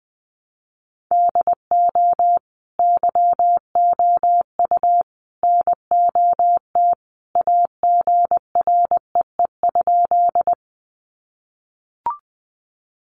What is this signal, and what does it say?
Morse code, heard as a keyed tone, about 20 wpm, 700 Hz. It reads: DO YOU NOT AGREE?
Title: Sentences from Top 1100 Words 20wpm